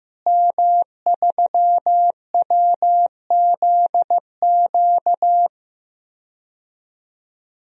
Morse code, heard as a keyed tone, about 15 wpm, 700 Hz. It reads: M3WZQ
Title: DXCC-100 Call Signs - 5 Characters - Repeated 2x - 15wpm